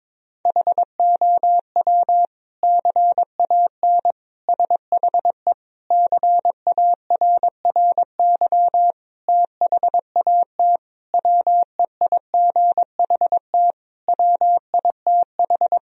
Morse code, heard as a keyed tone, about 22 wpm, 700 Hz. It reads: HOW CAN SHE CARRY THAT WEIGHT WITH
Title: Sentences from Top 1000 Words - Review Part 3 - 22wpm